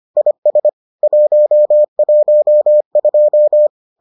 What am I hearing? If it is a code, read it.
Morse code, heard as a keyed tone, about 25 wpm, 600 Hz. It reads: IS 112